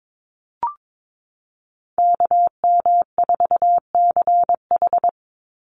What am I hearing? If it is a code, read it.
Morse code, heard as a keyed tone, about 22 wpm, 700 Hz. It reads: KM4CH
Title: Call Signs Repeated 2x - 22wpm